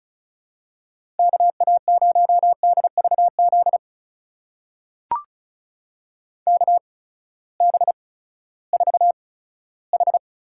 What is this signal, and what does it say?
Morse code, heard as a keyed tone, about 36 wpm, spaced as for 18 wpm, 700 Hz. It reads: KA0DVZ KB4H
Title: USA Call Signs - 6 Characters - 8x Character Spacing - 35wpm